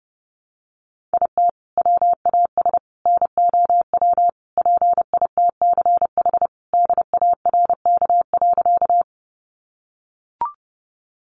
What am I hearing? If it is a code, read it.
Morse code, heard as a keyed tone, about 30 wpm, 700 Hz. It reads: IT WAS NOW PITCH DARK.
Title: Sentences from Top 1300 Words 30wpm